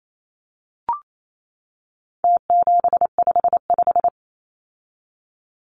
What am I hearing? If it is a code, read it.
Morse code, heard as a keyed tone, about 28 wpm, 700 Hz. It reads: T755
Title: Numbers - Contest Style - 4-Digit 28wpm